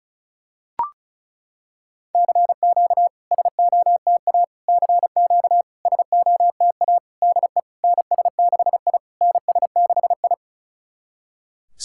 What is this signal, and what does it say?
Morse code, heard as a keyed tone, about 36 wpm, 700 Hz. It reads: CQ SOTA CQ SOTA DE NS6I NS6I
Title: SOTA QSO Practice 35wpm